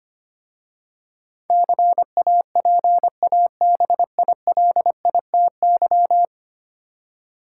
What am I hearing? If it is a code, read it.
Morse code, heard as a keyed tone, about 25 wpm, 700 Hz. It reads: CAPABILITY
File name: 10 Letter Words - First 3 Letters Given - 25wpm